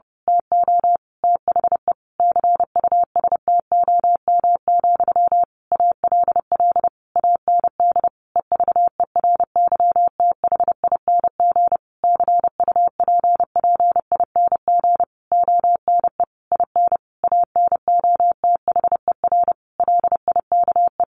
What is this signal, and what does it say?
Morse code, heard as a keyed tone, about 30 wpm, 700 Hz. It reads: TO THE CUSTOM, ALL AND EVERYTHING CUPPING ONE IN ANOTHER LIKE
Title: A Sound Of Thunder 30wpm